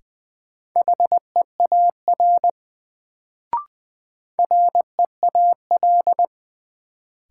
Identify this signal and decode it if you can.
Morse code, heard as a keyed tone, about 20 wpm, 700 Hz. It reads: HEAR REAL
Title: Sentences from Top 300 Words - New Word Drill - 20wpm